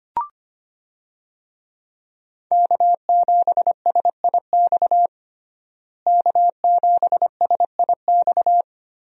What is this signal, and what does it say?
Morse code, heard as a keyed tone, about 25 wpm, 700 Hz. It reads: K7SIX K7SIX
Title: USA Call Signs - 5 Characters - Repeated 2x - 25wpm